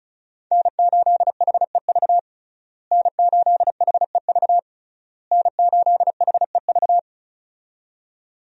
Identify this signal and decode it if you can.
Morse code, heard as a keyed tone, about 34 wpm, 700 Hz. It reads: N8HEV N8HEV N8HEV
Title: USA Call Signs - 5 Characters - Repeated 3x - 35wpm